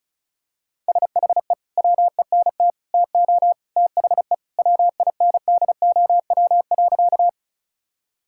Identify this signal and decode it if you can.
Morse code, heard as a keyed tone, about 34 wpm, 700 Hz. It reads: SHE WENT TO THE WINDOW.
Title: Sentences from Top 1600 Words 35wpm